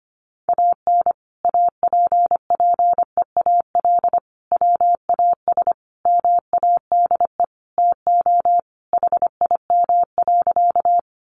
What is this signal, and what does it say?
Morse code, heard as a keyed tone, about 25 wpm, 700 Hz. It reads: AN APPEAL WAS MADE TO HIM.
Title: Sentences from Top 1300 Words 25wpm